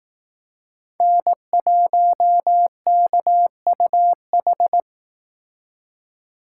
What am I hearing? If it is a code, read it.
Morse code, heard as a keyed tone, about 18 wpm, 700 Hz. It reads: N1KUH
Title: Speed Racing - USA 1x3 Call Signs 15wpm